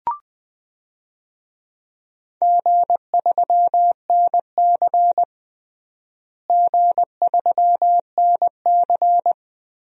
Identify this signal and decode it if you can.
Morse code, heard as a keyed tone, about 20 wpm, 700 Hz. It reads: G3NC G3NC
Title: DXCC-100 Call Signs - 4 Characters - Repeated 3x - 20wpm